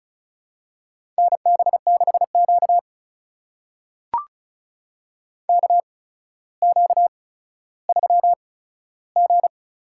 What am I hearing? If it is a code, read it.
Morse code, heard as a keyed tone, about 36 wpm, spaced as for 15 wpm, 700 Hz. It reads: NB6Q KQ3G
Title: USA Call Signs - 4 Characters - 8x Character Spacing - 35wpm